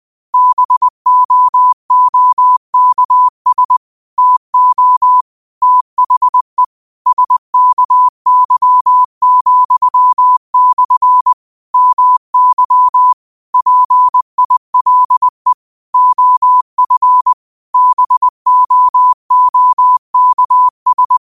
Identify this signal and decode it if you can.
Morse code, heard as a keyed tone, about 20 wpm, 1000 Hz. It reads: BOOKS TO THE SKY,/ MY PILE OF BOOKS
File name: qotdfiles / 20wpm